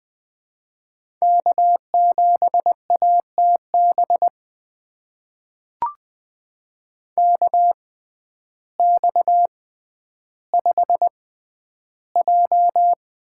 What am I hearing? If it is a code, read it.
Morse code, heard as a keyed tone, about 20 wpm, spaced as for 11 wpm, 700 Hz. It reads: K7ATB KX5J
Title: USA Call Signs - 6x Character Spacing - 20wpm